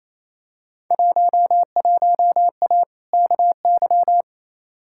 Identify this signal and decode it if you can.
Morse code, heard as a keyed tone, about 28 wpm, 700 Hz. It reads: 11A KY
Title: Field Day Exchanges Encore - 28wpm